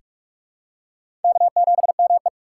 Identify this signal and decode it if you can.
Morse code, heard as a keyed tone, about 44 wpm, 690 Hz. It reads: K7ME
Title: USA Call Signs - 4 Characters - 3x Character Spacing - 45wpm